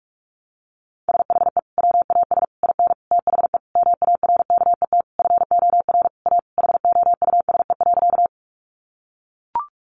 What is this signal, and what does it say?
Morse code, heard as a keyed tone, about 44 wpm, 700 Hz. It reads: SHE WAS IN THE MARKET FOR A HOUSE.
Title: Sentences from Top 600 Words 45wpm